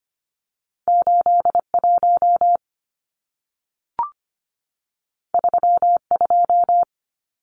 Morse code, heard as a keyed tone, about 25 wpm, 700 Hz. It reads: 81 32
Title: Numbers - 2-Digit 25wpm